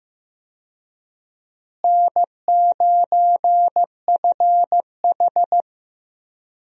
Morse code, heard as a keyed tone, about 16 wpm, 700 Hz. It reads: N9FH
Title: USA Call Signs - 4 Characters - Repeated 3x - 15wpm